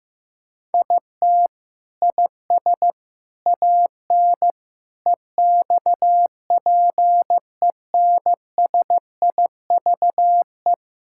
Morse code, heard as a keyed tone, about 16 wpm, 700 Hz. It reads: IT IS AN EXPENSIVE
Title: Sentences from Top 1000 Words - Review Part 6 - 15wpm